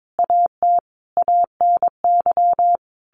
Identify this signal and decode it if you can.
Morse code, heard as a keyed tone, about 22 wpm, 700 Hz. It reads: AT ANY
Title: Sentences from Top 1500 Words 22wpm